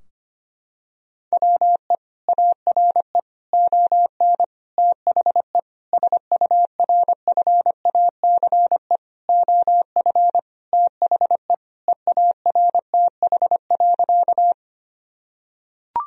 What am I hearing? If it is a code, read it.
Morse code, heard as a keyed tone, about 25 wpm, 700 Hz. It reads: WE ARE ON THE SURFACE OF THE EARTH.
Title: Sentences from Top 1000 Words - Review Part 3 - 25wpm